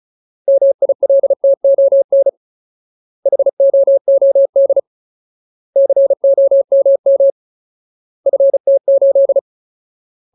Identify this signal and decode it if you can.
Morse code, heard as a keyed tone, about 34 wpm, 550 Hz. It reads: MILTON HOOD COMM FT8
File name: Ultimate Word List / 35WPM Ultimate